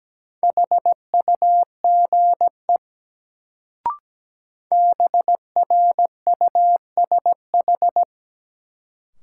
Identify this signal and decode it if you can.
Morse code, heard as a keyed tone, about 17 wpm, 700 Hz. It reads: HUGE BRUSH
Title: Sentences from Top 1100 Words - New Word Drill - 17wpm